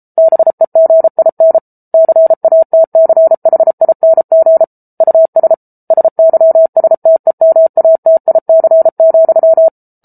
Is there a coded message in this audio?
Yes: Morse code from 0.2 to 9.7 s, heard as a keyed tone, about 34 wpm, 650 Hz. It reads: BEGIN CATCHING US SYSTEMATIC,